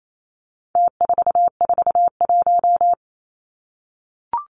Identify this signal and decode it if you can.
Morse code, heard as a keyed tone, about 28 wpm, 700 Hz. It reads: T441
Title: Numbers - Contest Style - 4-Digit 28wpm